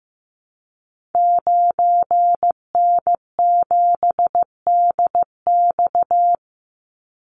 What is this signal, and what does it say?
Morse code, heard as a keyed tone, about 15 wpm, 700 Hz. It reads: 9N7DX
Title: Call Signs Repeated 3x - 15wpm